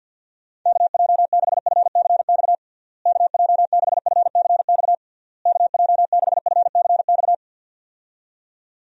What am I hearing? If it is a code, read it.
Morse code, heard as a keyed tone, about 52 wpm, 700 Hz. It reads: KJ6FCX KJ6FCX KJ6FCX
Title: USA Call Signs - 6 Characters - Repeated 3x - 50wpm